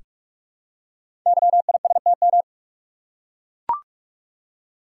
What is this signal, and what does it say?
Morse code, heard as a keyed tone, about 44 wpm, 700 Hz. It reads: YISTM
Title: Code Groups - 5 Letters - 45wpm